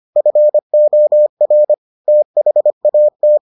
Morse code, heard as a keyed tone, about 25 wpm, 600 Hz. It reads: FOR THAT